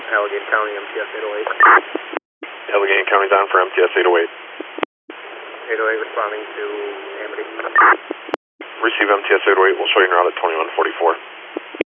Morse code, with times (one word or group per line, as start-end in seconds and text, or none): none